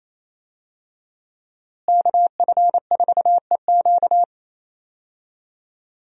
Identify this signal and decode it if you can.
Morse code, heard as a keyed tone, about 27 wpm, 700 Hz. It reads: KF4EQ
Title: USA Call Signs - 5 Characters - Repeated 2x - 28wpm